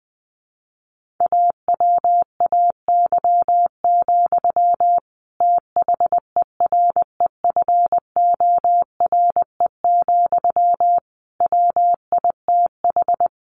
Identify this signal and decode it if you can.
Morse code, heard as a keyed tone, about 20 wpm, 700 Hz. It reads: AWAY, THEREFORE, WITH
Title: Sentences from Top 1500 Words 20wpm